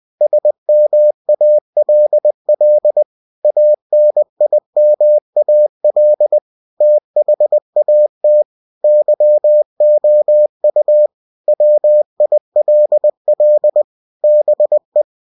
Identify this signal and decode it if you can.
Morse code, heard as a keyed tone, about 20 wpm, 600 Hz. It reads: SMALL ANIMAL THAT YOU WILL BE
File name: cw-books / winnie-the-pooh / winnie-20wpm-page-0062